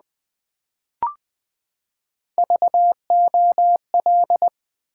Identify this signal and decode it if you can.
Morse code, heard as a keyed tone, about 20 wpm, 700 Hz. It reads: VOL
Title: Build A Word - 3 Letters At A Time - 9-Letter Words - 20wpm